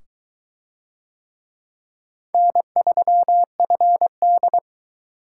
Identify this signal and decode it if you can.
Morse code, heard as a keyed tone, about 20 wpm, 700 Hz. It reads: N3FD